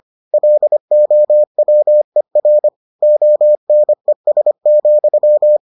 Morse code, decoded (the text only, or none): LOWER ONES,